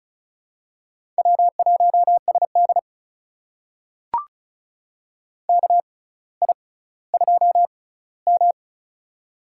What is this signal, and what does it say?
Morse code, heard as a keyed tone, about 34 wpm, spaced as for 18 wpm, 700 Hz. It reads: W1SD KI2M
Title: USA Call Signs - 6x Character Spacing - 35wpm